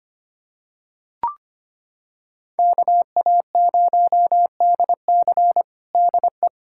Morse code, heard as a keyed tone, about 25 wpm, 700 Hz. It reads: KA0DC DE
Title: Practice QSO - Exchange 4 - Ending - 25wpm